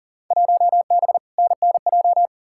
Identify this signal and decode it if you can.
Morse code, heard as a keyed tone, about 40 wpm, 700 Hz. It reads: 1B NNJ
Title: Field Day Exchanges - 40wpm